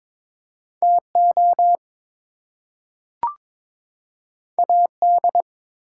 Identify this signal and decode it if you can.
Morse code, heard as a keyed tone, about 22 wpm, 700 Hz. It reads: TO AD
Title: Two Letter Words - 22wpm